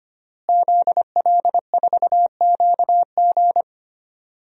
Morse code, heard as a keyed tone, about 25 wpm, 700 Hz. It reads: ZL4QG
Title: DXCC-100 Call Signs - 5 Characters - Repeated 2x - 25wpm